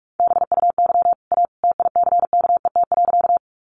Morse code, heard as a keyed tone, about 44 wpm, 700 Hz. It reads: BUY A TICKET.
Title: Sentences from Top 1400 Words 45wpm